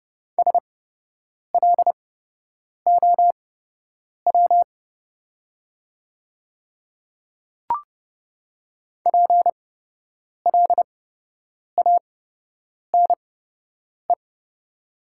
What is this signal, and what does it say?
Morse code, heard as a keyed tone, about 30 wpm, spaced as for 8 wpm, 700 Hz. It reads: SLOW PLANE
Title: 8x Character Spacing - Top 500 Words - 30wpm